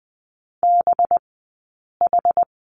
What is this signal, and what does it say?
Morse code, heard as a keyed tone, about 20 wpm, spaced as for 8 wpm, 700 Hz. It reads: BH
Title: 211 Lesson 34d - Number 8 - Two Characters 20wpm